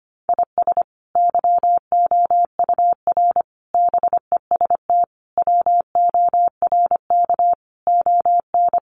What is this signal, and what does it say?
Morse code, heard as a keyed tone, about 25 wpm, 700 Hz. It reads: IS YOUR BEST WORK ON
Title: Sentences from Top 1200 Words 25wpm